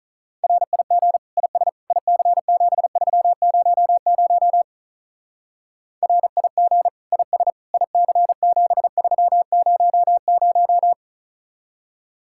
Morse code, heard as a keyed tone, about 44 wpm, 700 Hz. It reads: RIG IS IC7300 RIG IS IC7300